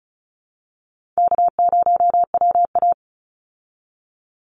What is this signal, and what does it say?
Morse code, heard as a keyed tone, about 34 wpm, 700 Hz. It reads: K0WA
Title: Easy Call Signs Repeated 2x - 35wpm